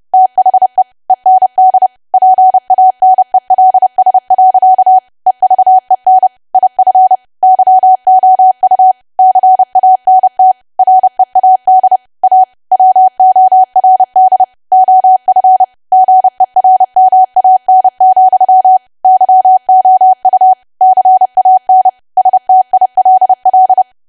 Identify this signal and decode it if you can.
Morse code, heard as a keyed tone, about 30 wpm, 750 Hz. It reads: THE END PANELS. EVEN IF YOU CANT READ A WORD OF GERMAN, YOU CAN STILL
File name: cwa / W1AW-Level3 / 30WPM